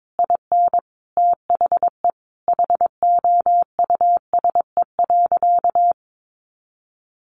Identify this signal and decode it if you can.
Morse code, heard as a keyed tone, about 22 wpm, 700 Hz. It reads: IN THE HOUSE.
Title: Sentences from Top 500 Words 22wpm